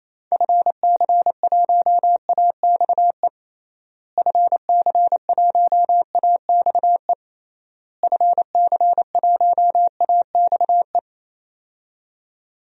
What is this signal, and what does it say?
Morse code, heard as a keyed tone, about 28 wpm, 700 Hz. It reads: FC1AXE FC1AXE FC1AXE
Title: DXCC-100 Call Signs - 6 Characters - Repeated 3x - 28wpm